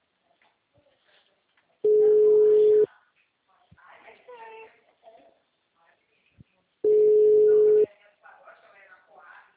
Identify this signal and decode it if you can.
Ringback tone (Europe, 425 Hz), with speech.
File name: out- -2008- - - .73